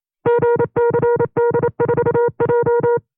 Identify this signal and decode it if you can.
Morse code, heard as a keyed tone, about 28 wpm, 480 Hz. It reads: GCD4J
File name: 25WPM